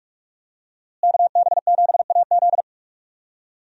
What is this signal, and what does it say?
Morse code, heard as a keyed tone, about 45 wpm, 700 Hz. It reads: KB7AZ
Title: USA Call Signs - 5 Characters - Repeated 3x - 45wpm